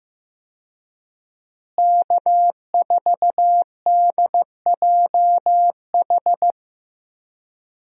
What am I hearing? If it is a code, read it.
Morse code, heard as a keyed tone, about 15 wpm, 700 Hz. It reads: K4DJH
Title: USA Call Signs - 5 Characters - Repeated 2x - 15wpm